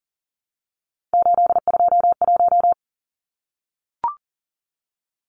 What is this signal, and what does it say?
Morse code, heard as a keyed tone, about 40 wpm, 700 Hz. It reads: 821
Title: Numbers - 3-Digit 40wpm